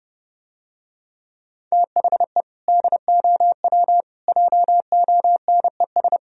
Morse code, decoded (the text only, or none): THE DOW JONES